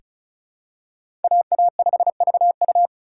Morse code, heard as a keyed tone, about 36 wpm, 700 Hz. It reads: AA5VU
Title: K1USN SST QSO Practice - 35wpm